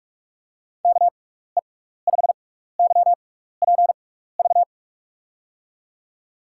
Morse code, heard as a keyed tone, about 44 wpm, spaced as for 15 wpm, 700 Hz. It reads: KE5YPV
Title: USA Call Signs - 6 Characters - 6x Character Spacing - 45wpm